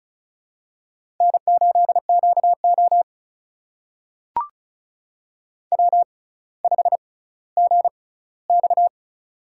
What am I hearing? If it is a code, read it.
Morse code, heard as a keyed tone, about 36 wpm, spaced as for 18 wpm, 700 Hz. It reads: N8QO W5GX
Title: USA Call Signs - 4 Characters - 6x Character Spacing - 35wpm